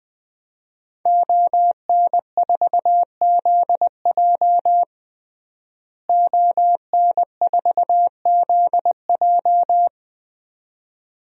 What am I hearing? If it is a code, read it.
Morse code, heard as a keyed tone, about 20 wpm, 700 Hz. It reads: ON4ZJ ON4ZJ
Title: DXCC-100 Call Signs - 5 Characters - Repeated 2x - 20wpm